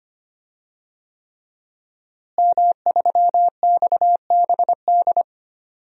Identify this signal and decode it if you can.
Morse code, heard as a keyed tone, about 25 wpm, 700 Hz. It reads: M3XBD
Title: DXCC-100 Call Signs - 5 Characters - Repeated 3x - 25wpm